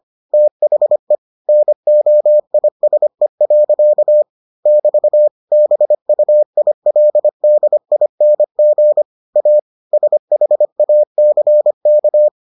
Morse code, heard as a keyed tone, about 25 wpm, 600 Hz. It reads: THE NOISE. <BT> BUILDING A SHACK